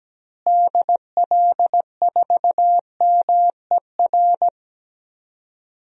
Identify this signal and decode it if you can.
Morse code, heard as a keyed tone, about 17 wpm, 700 Hz. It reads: DL4MER
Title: DXCC-100 Call Signs - 6 Characters - Repeated 2x - 17wpm